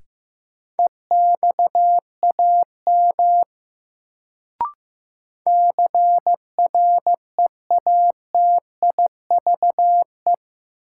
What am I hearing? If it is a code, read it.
Morse code, heard as a keyed tone, about 15 wpm, 700 Hz. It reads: EXAM CREATIVE
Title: Sentences from Top 1000 Words - Review Part 5 - New Word Drill - 15wpm